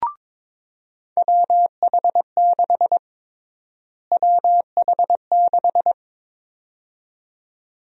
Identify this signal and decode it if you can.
Morse code, heard as a keyed tone, about 22 wpm, 700 Hz. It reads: WH6 WH6
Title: USA Call Sign Prefixes - Repeated 2x - 22wpm